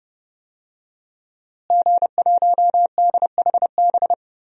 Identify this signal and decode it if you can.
Morse code, heard as a keyed tone, about 30 wpm, 700 Hz. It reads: G1DHB
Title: DXCC-100 Call Signs - 5 Characters - Repeated 3x - 30wpm